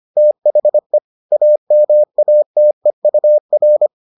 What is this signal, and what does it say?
Morse code, heard as a keyed tone, about 25 wpm, 600 Hz. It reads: THE AMATEUR